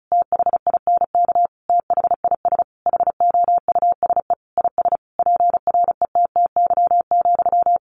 Morse code, heard as a keyed tone, about 34 wpm, 700 Hz. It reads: THINK THIS HOUSE IS PRETTY,
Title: Sentences from Top 1000 Words 35wpm